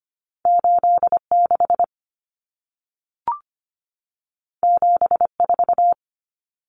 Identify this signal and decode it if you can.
Morse code, heard as a keyed tone, about 25 wpm, 700 Hz. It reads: 86 74
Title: Common Cut Numbers - 2-Digit 25wpm